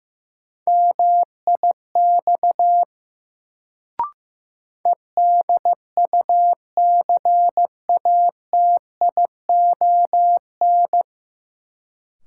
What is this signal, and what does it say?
Morse code, heard as a keyed tone, about 15 wpm, 700 Hz. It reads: MIX EDUCATION
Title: Sentences from Top 1100 Words - New Word Drill Alt - 15wpm